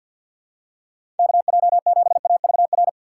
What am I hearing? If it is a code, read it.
Morse code, heard as a keyed tone, about 52 wpm, 700 Hz. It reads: KJ7AVR
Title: USA Call Signs - 6 Characters - 3x Character Spacing - 50wpm